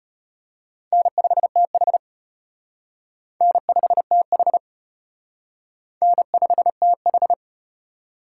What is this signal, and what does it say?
Morse code, heard as a keyed tone, about 34 wpm, 700 Hz. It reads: N5TH N5TH N5TH